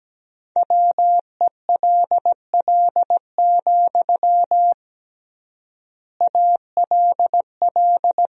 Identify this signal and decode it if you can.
Morse code, heard as a keyed tone, about 17 wpm, 700 Hz. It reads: WELL, ALL
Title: 3x Word-Spacing - Sentences from Top 200 Words - 17wpm